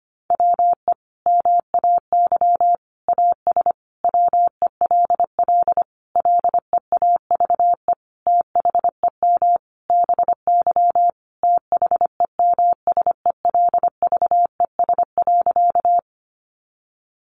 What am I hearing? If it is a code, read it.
Morse code, heard as a keyed tone, about 25 wpm, 700 Hz. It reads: WE MAY AS WELL LEAVE THEM BY THEMSELVES.
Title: Sentences from Top 800 Words 25wpm